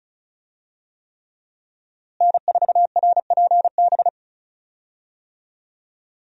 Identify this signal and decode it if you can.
Morse code, heard as a keyed tone, about 34 wpm, 700 Hz. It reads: N4RPB
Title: USA Call Signs - 5 Characters - Repeated 3x - 35wpm